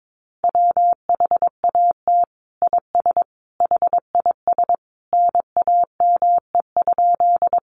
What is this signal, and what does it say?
Morse code, heard as a keyed tone, about 22 wpm, 700 Hz. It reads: WHAT IS HIS NAME?
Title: Sentences from Top 200 Words 22wpm